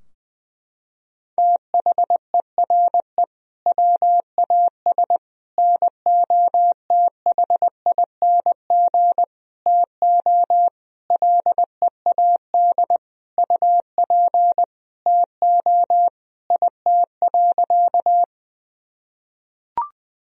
Morse code, heard as a keyed tone, about 20 wpm, 700 Hz. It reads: THERE WAS NOTHING TO LEAD UP TO IT.
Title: Sentences from Top 1000 Words - Review Part 2 - 20wpm